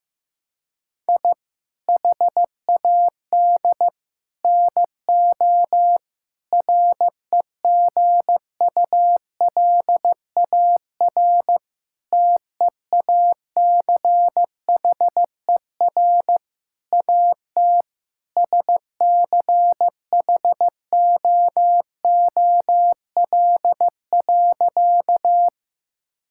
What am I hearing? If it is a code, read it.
Morse code, heard as a keyed tone, about 15 wpm, 700 Hz. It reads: I HAD NO REGULAR TEACHER AT SCHOOL.
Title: Sentences from Top 1000 Words - Review Part 6 - 15wpm